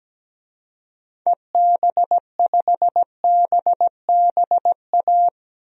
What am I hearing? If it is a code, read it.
Morse code, heard as a keyed tone, about 17 wpm, 700 Hz. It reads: EB5BBA
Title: DXCC-100 Call Signs - 6 Characters - Repeated 3x - 17wpm